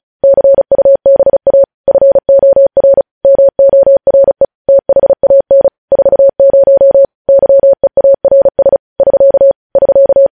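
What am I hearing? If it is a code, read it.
Morse code, heard as a keyed tone, about 36 wpm, 550 Hz. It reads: CUBA FOR MORE THAN 40 YEARS <SK> <SK>
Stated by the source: News headlines sent in Morse at speed 35.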